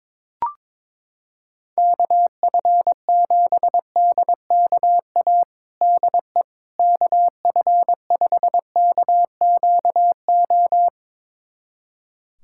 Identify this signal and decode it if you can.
Morse code, heard as a keyed tone, about 22 wpm, 700 Hz. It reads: KF7DKA DE KF5KQO
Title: USA Callsign DE Callsign - 6 Characters - 22wpm